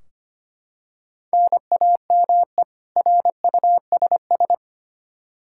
Morse code, heard as a keyed tone, about 25 wpm, 700 Hz. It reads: NAME RUSS
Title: K1USN SST QSO Practice 25wpm